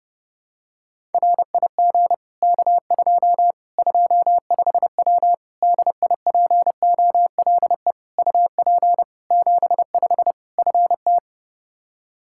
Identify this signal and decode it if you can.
Morse code, heard as a keyed tone, about 30 wpm, 700 Hz. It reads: RIG K2 25W DIPOLE UP 75 FT
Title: Practice QSO - Exchange 2 - Info - 30wpm